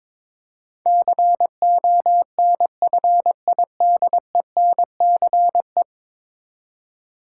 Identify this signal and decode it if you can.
Morse code, heard as a keyed tone, about 22 wpm, 700 Hz. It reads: CONFIDENCE
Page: morse-code-ninja-practice-22wpm